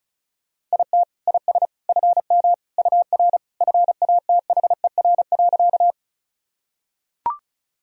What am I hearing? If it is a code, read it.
Morse code, heard as a keyed tone, about 34 wpm, 700 Hz. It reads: IT IS FM UR FATHER.
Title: Sentences from Top 300 Words and Abbreviations 35wpm